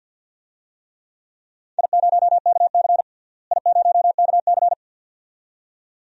Morse code, heard as a keyed tone, about 48 wpm, 700 Hz. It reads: I0KC I0KC
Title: DXCC-100 Call Signs - 4 Characters - Repeated 2x - 50wpm